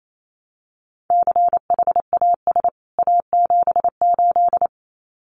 Morse code, heard as a keyed tone, about 28 wpm, 700 Hz. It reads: CHAS A78
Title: CWT Exchange 28wpm